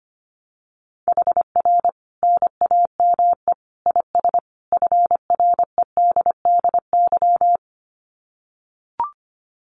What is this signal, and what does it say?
Morse code, heard as a keyed tone, about 25 wpm, 700 Hz. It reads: HR NAME IS FREDDY
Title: Common QSO Phrases 25wpm